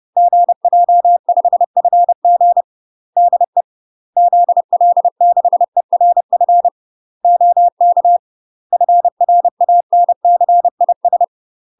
Morse code, heard as a keyed tone, about 30 wpm, 700 Hz. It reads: GJ5FG DE ZL6ERF OK FRANCIS
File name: UK MASTER QSOs TRAINING / 30WPM / UK MASTER QSOs 0000